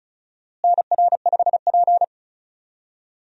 Morse code, heard as a keyed tone, about 34 wpm, 700 Hz. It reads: NR5P
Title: USA Call Signs - 4 Characters - Repeated 3x - 35wpm